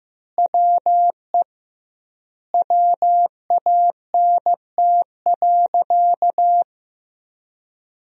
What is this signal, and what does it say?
Morse code, heard as a keyed tone, about 15 wpm, 700 Hz. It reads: WE WANT.
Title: 2x Word-Spacing - Sentences from Top 200 Words - 15wpm